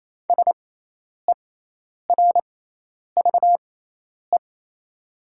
Morse code, heard as a keyed tone, about 28 wpm, spaced as for 10 wpm, 700 Hz. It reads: SERVE
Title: 6x Character Spacing - Top 500 Words - 28wpm